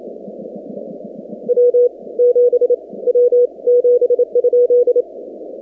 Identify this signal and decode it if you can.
Morse code, heard as a keyed tone, about 27 wpm, 490 Hz. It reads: W 7 W 7?
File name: 2019-cw-wpx-kt5j-k5pi